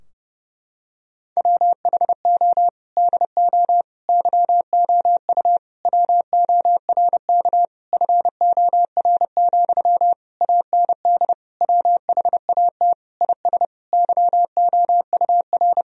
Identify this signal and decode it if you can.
Morse code, heard as a keyed tone, about 30 wpm, 700 Hz. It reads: WHO DO YOU WORK FOR, AND WHAT IS YOUR
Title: Sentences from Top 1000 Words - Review Part 4 - 30wpm